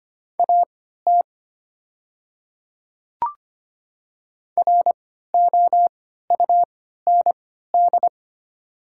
Morse code, heard as a keyed tone, about 25 wpm, spaced as for 14 wpm, 700 Hz. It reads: AT ROUND
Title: 3x Character Spacing - Top 300 Words - 25wpm